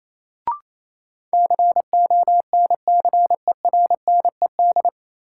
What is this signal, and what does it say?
Morse code, heard as a keyed tone, about 28 wpm, 700 Hz. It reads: CONCERNED
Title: Sentences from Top 1000 Words - Review Part 6 - New Word Drill - 28wpm